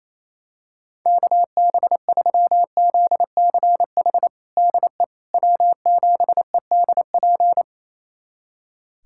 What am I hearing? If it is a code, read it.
Morse code, heard as a keyed tone, about 27 wpm, 700 Hz. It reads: KB3ZCH DE W7EDP
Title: USA Callsign DE Callsign 28wpm